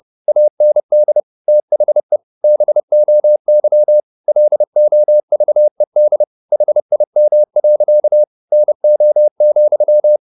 Morse code, heard as a keyed tone, about 30 wpm, 600 Hz. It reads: AND THE BOY LOVED HIM. NO,
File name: the old man and the sea / the old man and the sea 1 30wpm